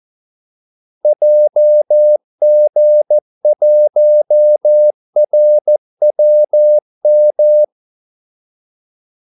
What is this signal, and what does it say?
Morse code, heard as a keyed tone, about 14 wpm, 600 Hz. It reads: JG1RWM
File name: Callsigns-14wpm